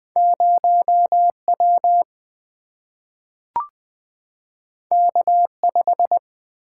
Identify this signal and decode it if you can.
Morse code, heard as a keyed tone, about 20 wpm, 700 Hz. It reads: 0 W K 5
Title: Single-Letter Single-Number 20wpm